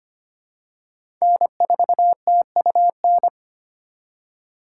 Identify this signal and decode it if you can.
Morse code, heard as a keyed tone, about 25 wpm, 700 Hz. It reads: N4TUN
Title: USA Call Signs - 5 Characters - 4x Character Spacing - 25wpm